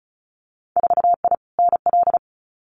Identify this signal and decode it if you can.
Morse code, heard as a keyed tone, about 36 wpm, 700 Hz. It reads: 4I NL
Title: Winter Field Day Exchanges Encore 35wpm